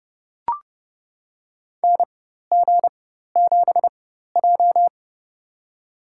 Morse code, heard as a keyed tone, about 31 wpm, spaced as for 14 wpm, 700 Hz. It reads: NG7J
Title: USA Call Signs - 4 Characters - 4x Character Spacing - 30wpm